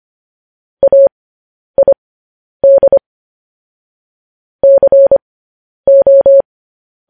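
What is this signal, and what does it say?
Morse code, heard as a keyed tone, about 25 wpm, spaced as for 10 wpm, 550 Hz. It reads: AID CO
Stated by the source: Morse code news headlines broadcast at 10 wpm.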